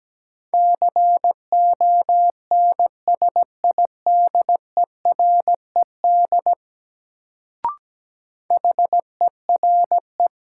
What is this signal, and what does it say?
Morse code, heard as a keyed tone, about 17 wpm, 700 Hz. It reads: CONSIDERED HERE
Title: Sets of 5 Words - New Word Drill - 17wpm